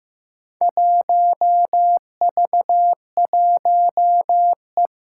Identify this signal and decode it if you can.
Morse code, heard as a keyed tone, about 15 wpm, 700 Hz. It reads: 1V1E
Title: Four Challenging Letter-Number Combinations 15wpm